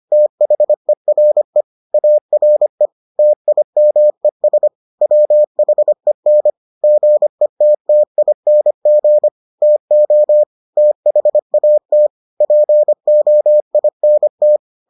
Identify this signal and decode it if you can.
Morse code, heard as a keyed tone, about 25 wpm, 600 Hz. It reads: THERE ARE TIMES WHEN GETTING TO THAT POINT